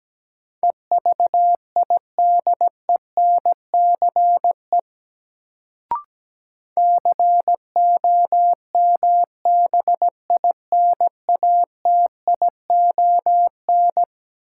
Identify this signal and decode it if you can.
Morse code, heard as a keyed tone, about 17 wpm, 700 Hz. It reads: EVIDENCE COMBINATION
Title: Sentences from Top 1300 Words - New Word Drill Alt - 17wpm